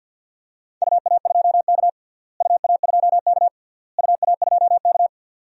Morse code, heard as a keyed tone, about 47 wpm, 700 Hz. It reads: UA2K UA2K UA2K
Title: DXCC-100 Call Signs - 4 Characters - Repeated 3x - 50wpm